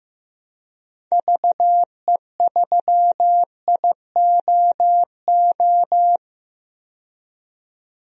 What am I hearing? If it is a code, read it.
Morse code, heard as a keyed tone, about 15 wpm, 700 Hz. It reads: VE3IOO
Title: DXCC-100 Call Signs - 6 Characters - 15wpm